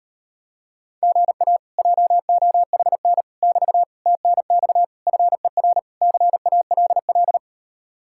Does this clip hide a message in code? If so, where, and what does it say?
Yes: Morse code from 1.0 to 7.4 s, heard as a keyed tone, about 39 wpm, 700 Hz. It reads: GA JOHN <BT> TNX FER CALL